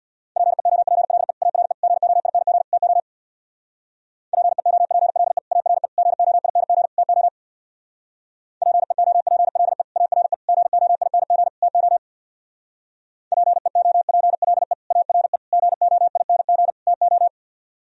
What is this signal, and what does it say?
Morse code, heard as a keyed tone, about 62 wpm, 700 Hz. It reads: PEOPLE ARE GOING TO PEOPLE ARE GOING TO PEOPLE ARE GOING TO PEOPLE ARE GOING TO
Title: Speed-Racing - Sets of 4 Words 50wpm